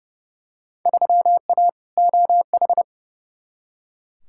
Morse code, heard as a keyed tone, about 30 wpm, 700 Hz. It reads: 3A OH
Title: Field Day Exchanges - 30wpm